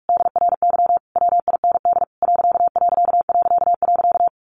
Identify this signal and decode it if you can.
Morse code, heard as a keyed tone, about 44 wpm, 700 Hz. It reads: DRY WIND ....
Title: A Sound Of Thunder 45wpm